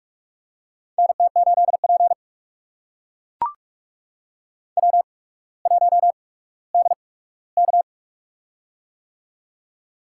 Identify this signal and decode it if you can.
Morse code, heard as a keyed tone, about 44 wpm, spaced as for 19 wpm, 700 Hz. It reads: NT8P W1DK
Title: USA Call Signs - 8x Character Spacing - 45wpm